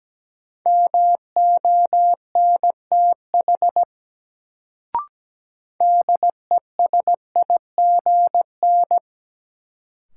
Morse code, heard as a keyed tone, about 17 wpm, 700 Hz. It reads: MONTH DESIGN
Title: Sentences from Top 800 Words - New Word Drill - 17wpm